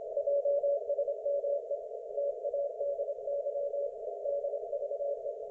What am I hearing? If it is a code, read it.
Morse code, heard as a keyed tone, about 27 wpm, 570 Hz. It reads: JSME TADY DVE
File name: Archiv / CUC216 / ok1fhi cul216